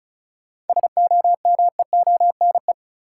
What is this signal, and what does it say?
Morse code, heard as a keyed tone, about 36 wpm, 700 Hz. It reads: SOMEONE
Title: Sentences from Top 600 Words - New Word Drill - 35wpm